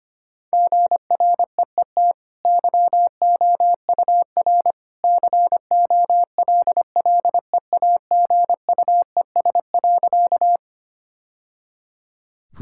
Morse code, heard as a keyed tone, about 25 wpm, 700 Hz. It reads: GREET YOUR COLLEAGUES.
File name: Easy Sentences with Hints - 25wpm